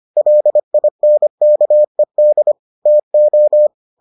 Morse code, heard as a keyed tone, about 25 wpm, 600 Hz. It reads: LINKED TO